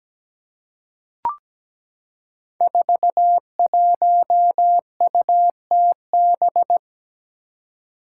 Morse code, heard as a keyed tone, about 17 wpm, 700 Hz. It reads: 41UTB
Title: Five Challenging Letter-Number Combinations 17wpm